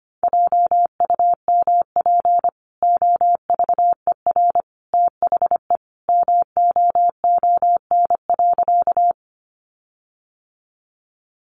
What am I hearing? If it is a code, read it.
Morse code, heard as a keyed tone, about 25 wpm, 700 Hz. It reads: JUMP OVER THE MOON.
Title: Sentences from Top 1300 Words 25wpm